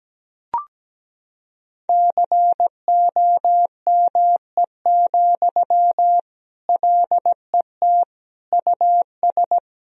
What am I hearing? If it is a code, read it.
Morse code, heard as a keyed tone, about 17 wpm, 700 Hz. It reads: COME, LET US
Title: Sentences from Top 1000 Words - Review Part 2 - 17wpm